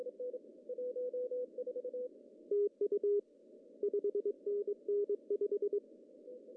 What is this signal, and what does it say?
Morse code, heard as a keyed tone, about 23 wpm, 410 Hz. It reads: TU 5NN 5
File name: CW / cqww-cw-k5tr